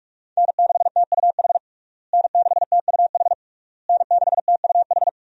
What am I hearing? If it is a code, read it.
Morse code, heard as a keyed tone, about 46 wpm, 700 Hz. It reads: N6TUH N6TUH N6TUH
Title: USA Call Signs - 5 Characters - Repeated 3x - 45wpm